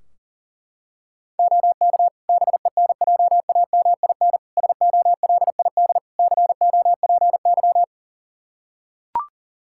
Morse code, heard as a keyed tone, about 40 wpm, 700 Hz. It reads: OK BENJAMIN SOLID COPY
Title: Practice QSO - Exchange 3 - More Info - 40wpm